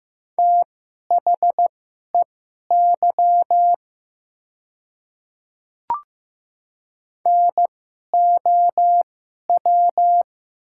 Morse code, heard as a keyed tone, about 15 wpm, spaced as for 11 wpm, 700 Hz. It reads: THEY NOW
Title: 2x Character Spacing - Top 100 Words - 15wpm